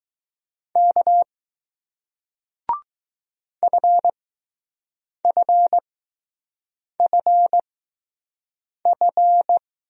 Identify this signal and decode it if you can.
Morse code, heard as a keyed tone, about 20 wpm, spaced as for 6 wpm, 700 Hz. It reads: K FFFF